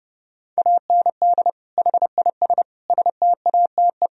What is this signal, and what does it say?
Morse code, heard as a keyed tone, about 28 wpm, 700 Hz. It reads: AND HIS STATE
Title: Sentences from Top 1000 Words - Review Part 2 - 30wpm